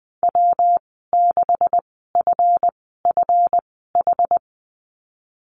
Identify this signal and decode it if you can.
Morse code, heard as a keyed tone, about 20 wpm, spaced as for 15 wpm, 700 Hz. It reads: W6FFH
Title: 167 Lesson 27g - Number 7 - Call Signs 20wpm